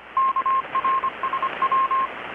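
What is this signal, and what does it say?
Morse code, heard as a keyed tone, about 25 wpm, 1060 Hz, with radio noise. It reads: KRSW
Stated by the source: Morse from KPH's Night of Nights XX maritime radiotelegraphy event, received in Chiapas, Mexico about 3,800 km away on a Kenwood R-600 with a grounded 10 m long-wire antenna.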